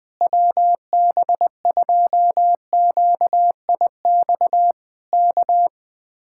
Morse code, heard as a keyed tone, about 20 wpm, 700 Hz. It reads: WB2QIX K
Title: Speed-Racing Common QSO Phrases Encore - 17wpm